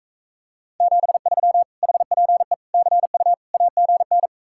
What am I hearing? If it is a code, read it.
Morse code, heard as a keyed tone, about 45 wpm, 700 Hz. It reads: 73 HPE CU AGN